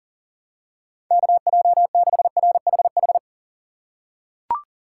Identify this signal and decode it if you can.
Morse code, heard as a keyed tone, about 37 wpm, 700 Hz. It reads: KJ6RHH
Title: USA Call Signs - 6 Characters - 3x Character Spacing - 40wpm